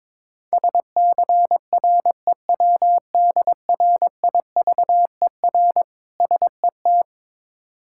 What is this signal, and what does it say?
Morse code, heard as a keyed tone, about 21 wpm, 700 Hz. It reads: SCREWDRIVER SET
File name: Difficult Words with Hints - 22wpm